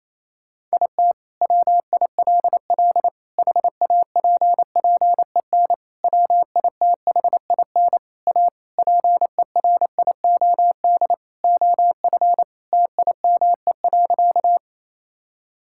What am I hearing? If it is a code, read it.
Morse code, heard as a keyed tone, about 28 wpm, 700 Hz. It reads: IT WILL HAPPEN WITHIN A PERIOD OF TIME.
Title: Sentences from Top 1000 Words - Review Part 5 - 28wpm